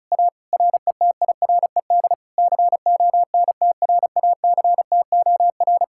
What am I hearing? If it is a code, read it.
Morse code, heard as a keyed tone, about 40 wpm, 700 Hz. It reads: A RETIRED CONTRACTOR